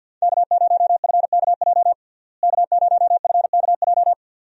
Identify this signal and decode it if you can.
Morse code, heard as a keyed tone, about 46 wpm, 700 Hz. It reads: K0FKJ K0FKJ
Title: USA Call Signs - 5 Characters - Repeated 2x - 50wpm